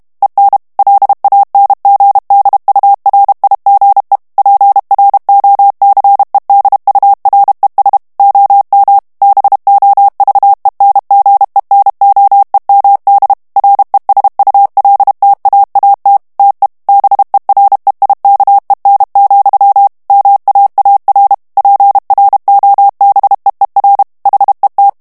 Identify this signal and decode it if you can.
Morse code, heard as a keyed tone, about 32 wpm, 800 Hz. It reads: EN LANGDURIGE PROCEDURES OM BOVENGENOEMD RESULTAAT TE BEREIKEN, MAAR PROBEER HET
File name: HOLLANDAIS texte clair 2 vitesse 30 mots minute